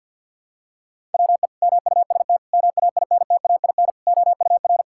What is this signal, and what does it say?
Morse code, heard as a keyed tone, about 47 wpm, 700 Hz. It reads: WE MUST MAINTAIN OUR
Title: Sentences from Top 1000 Words - Review Part 5 - 50wpm